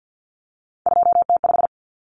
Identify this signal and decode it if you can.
Morse code, heard as a keyed tone, about 50 wpm, 700 Hz. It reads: 2T5
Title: Common Cut Numbers - 3-Digit 50wpm